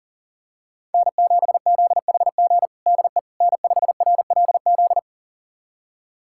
Morse code, heard as a keyed tone, about 40 wpm, 700 Hz. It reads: N7ZHG DE N5RLZ
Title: USA Callsign DE Callsign - 5 Characters - 40wpm